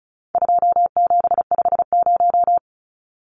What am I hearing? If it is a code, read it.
Morse code, heard as a keyed tone, about 34 wpm, 700 Hz. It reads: 2750
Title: Numbers - 4-Digit 35wpm